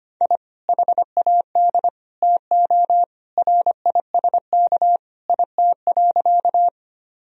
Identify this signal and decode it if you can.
Morse code, heard as a keyed tone, about 25 wpm, 700 Hz. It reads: I HAD TO RISK IT.
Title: Sentences from Top 1000 Words - Review Part 4 - 25wpm